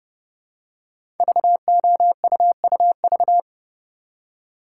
Morse code, heard as a keyed tone, about 30 wpm, 700 Hz. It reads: VOUUV
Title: Code Groups - 5 Letters - 30wpm